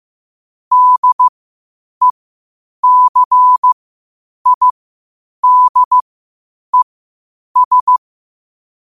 Morse code, heard as a keyed tone, about 15 wpm, spaced as for 8 wpm, 1000 Hz. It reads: DECIDES
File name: qotdfiles / 8wpm